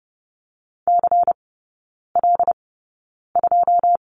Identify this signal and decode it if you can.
Morse code, heard as a keyed tone, about 30 wpm, spaced as for 9 wpm, 700 Hz. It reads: CL2
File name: Lesson 16e - Letter c - Three Characters 30wpm